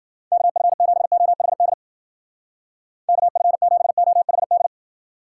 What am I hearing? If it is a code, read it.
Morse code, heard as a keyed tone, about 52 wpm, 700 Hz. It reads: KF7OHD KF7OHD